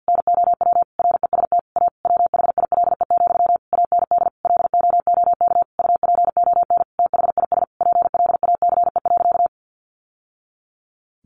Morse code, heard as a keyed tone, about 50 wpm, 700 Hz. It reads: NOW REST A WHILE, AND LOOK UPON THIS PLACE.
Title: Sentences from Top 1100 Words 50wpm